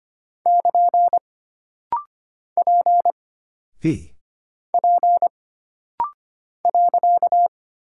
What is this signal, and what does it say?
Morse code, heard as a keyed tone, about 25 wpm, 700 Hz. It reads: <KN> P P .
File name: Single Character (Letter:Number:Punctuation:Prosign) - Rapid-Fire - 25wpm